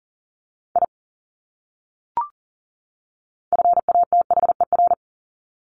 Morse code, heard as a keyed tone, about 38 wpm, 700 Hz. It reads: I FATHER
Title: Top 200 Words 40wpm